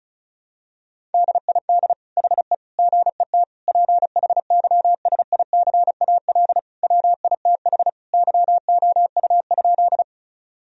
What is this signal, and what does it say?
Morse code, heard as a keyed tone, about 34 wpm, 700 Hz. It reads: DID HE GET PHYSICAL WITH YOU?
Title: Sentences from Top 1000 Words - Review Part 4 - 35wpm